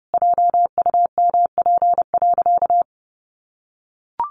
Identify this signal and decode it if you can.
Morse code, heard as a keyed tone, about 30 wpm, 700 Hz. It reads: JUMP.
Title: Sentences from Top 1300 Words 30wpm